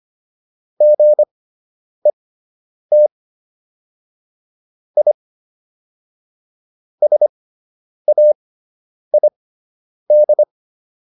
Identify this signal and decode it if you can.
Morse code, heard as a keyed tone, about 25 wpm, spaced as for 9 wpm, 600 Hz. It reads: GET I SAID